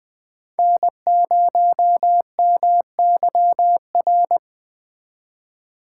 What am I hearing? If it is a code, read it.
Morse code, heard as a keyed tone, about 20 wpm, 700 Hz. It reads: N0MYR
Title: Speed Racing - USA 1x3 Call Signs 17wpm